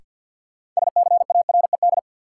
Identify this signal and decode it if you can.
Morse code, heard as a keyed tone, about 47 wpm, 700 Hz. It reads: SCARED
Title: Sentences from Top 700 Words - New Word Drill - 50wpm